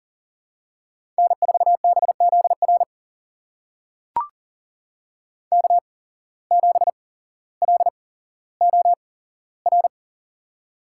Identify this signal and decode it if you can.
Morse code, heard as a keyed tone, about 40 wpm, spaced as for 17 wpm, 700 Hz. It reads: N4BZR K7LOR
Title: USA Call Signs - 5 Characters - 8x Character Spacing - 40wpm